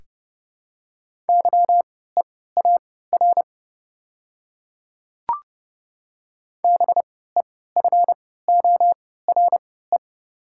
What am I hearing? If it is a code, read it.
Morse code, heard as a keyed tone, about 30 wpm, spaced as for 17 wpm, 700 Hz. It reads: YEAR BEFORE
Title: 3x Character Spacing - Top 200 Words - 30wpm